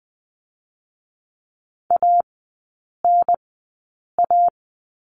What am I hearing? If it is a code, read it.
Morse code, heard as a keyed tone, about 20 wpm, spaced as for 8 wpm, 700 Hz. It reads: ANA
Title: 047 Lesson 9e - Number 4 - Three Characters 20wpm